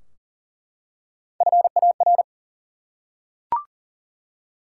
Morse code, heard as a keyed tone, about 37 wpm, 700 Hz. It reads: FAR
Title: Three Letter Words - 40wpm